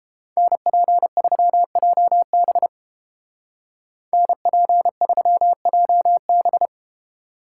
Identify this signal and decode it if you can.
Morse code, heard as a keyed tone, about 32 wpm, 700 Hz. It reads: NP3JB NP3JB